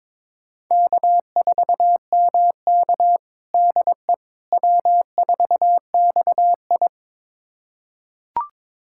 Morse code, heard as a keyed tone, about 22 wpm, 700 Hz. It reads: K4MK DE W4XI
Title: USA Callsign DE Callsign - 4 Characters - 22wpm